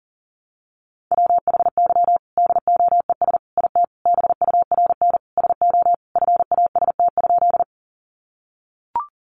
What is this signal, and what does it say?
Morse code, heard as a keyed tone, about 40 wpm, 700 Hz. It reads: WHY DOES IT BURN SO FAST?
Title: Sentences from Top 1300 Words 40wpm